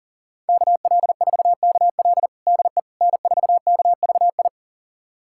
Morse code, heard as a keyed tone, about 40 wpm, 700 Hz. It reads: KL4KL DE N4KVI
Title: USA Callsign DE Callsign - 5 Characters - 40wpm